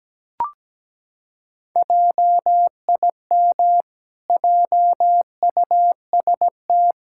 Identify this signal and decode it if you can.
Morse code, heard as a keyed tone, about 17 wpm, 700 Hz. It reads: JIM JUST
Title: Pangram Challenge v2 - 17wpm